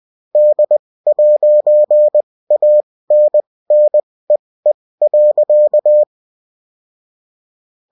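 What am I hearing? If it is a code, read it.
Morse code, heard as a keyed tone, about 20 wpm, spaced as for 16 wpm, 600 Hz. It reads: D'ANNEE.
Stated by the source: UFT bulletin sent in CW.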